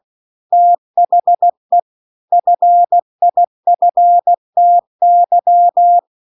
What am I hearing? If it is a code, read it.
Morse code, heard as a keyed tone, about 16 wpm, 700 Hz. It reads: THE FIFTY
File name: CW-The-Big-Twoheaded-River-2-16WPM